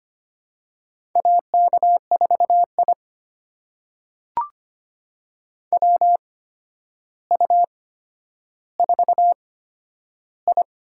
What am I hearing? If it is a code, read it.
Morse code, heard as a keyed tone, about 25 wpm, spaced as for 11 wpm, 700 Hz. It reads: AK4I WU4I
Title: USA Call Signs - 4 Characters - 8x Character Spacing - 25wpm